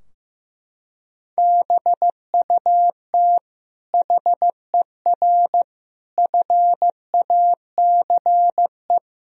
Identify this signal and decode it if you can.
Morse code, heard as a keyed tone, about 15 wpm, 700 Hz. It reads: BUT HER FACE
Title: Sentences from Top 1000 Words - Review Part 2 - 15wpm